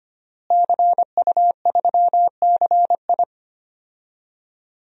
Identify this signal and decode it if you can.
Morse code, heard as a keyed tone, about 25 wpm, 700 Hz. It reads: CU3CI
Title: DXCC-100 Call Signs - 5 Characters - Repeated 2x - 25wpm